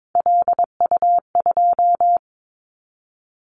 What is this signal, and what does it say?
Morse code, heard as a keyed tone, about 22 wpm, 700 Hz. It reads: LU2
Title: Callsign Prefixes for Top 100 DXCC - 22wpm